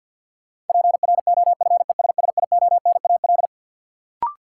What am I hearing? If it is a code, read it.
Morse code, heard as a keyed tone, about 52 wpm, 700 Hz. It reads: PROFESSIONAL
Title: Sentences from Top 1400 Words - New Word Drill Alt - 50wpm